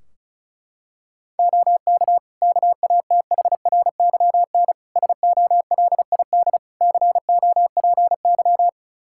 Morse code, heard as a keyed tone, about 34 wpm, 700 Hz. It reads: OK KATHRYN SOLID COPY
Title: K1USN SST QSO Practice 35wpm